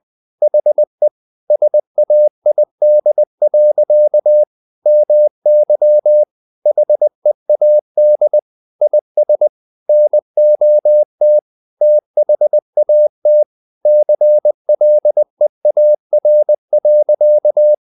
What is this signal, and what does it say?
Morse code, heard as a keyed tone, about 20 wpm, 600 Hz. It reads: HE SAID. MY HEAD IS NOT THAT CLEAR.
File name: the old man and the sea part 2 20wpm